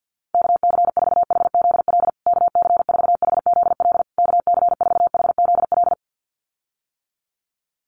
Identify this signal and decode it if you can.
Morse code, heard as a keyed tone, about 50 wpm, 700 Hz. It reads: KC4HZL KC4HZL KC4HZL
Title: USA Call Signs Repeated 3x 50wpm